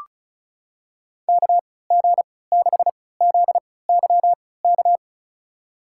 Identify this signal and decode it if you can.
Morse code, heard as a keyed tone, about 36 wpm, spaced as for 20 wpm, 700 Hz. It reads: KG6ZYK
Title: USA Call Signs - 3x Character Spacing - 35wpm